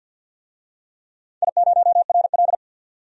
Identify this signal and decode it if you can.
Morse code, heard as a keyed tone, about 45 wpm, 700 Hz. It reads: I0RL
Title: DXCC-100 Call Signs - 4 Characters - 50wpm